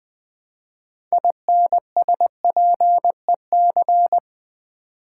Morse code, heard as a keyed tone, about 20 wpm, 700 Hz. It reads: INSPEC
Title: Build A Word - 3 Letters At A Time - 9-Letter Words - 20wpm